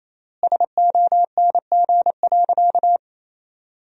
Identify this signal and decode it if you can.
Morse code, heard as a keyed tone, about 28 wpm, 700 Hz. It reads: SONG.
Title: 4x Word-Spacing - Sentences from Top 400 Words - 28wpm